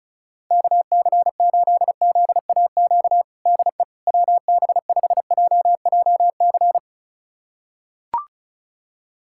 Morse code, heard as a keyed tone, about 36 wpm, 700 Hz. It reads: KC8ZAQ DE WB5JJC
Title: USA Callsign DE Callsign 35wpm